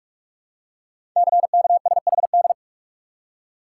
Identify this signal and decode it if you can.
Morse code, heard as a keyed tone, about 44 wpm, 700 Hz. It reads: CKSHD
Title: Code Groups - 5 Letters - 45wpm